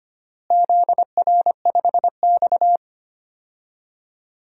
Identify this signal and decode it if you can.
Morse code, heard as a keyed tone, about 25 wpm, 700 Hz. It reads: ZR5X
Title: DXCC-100 Call Signs - 4 Characters - Repeated 3x - 25wpm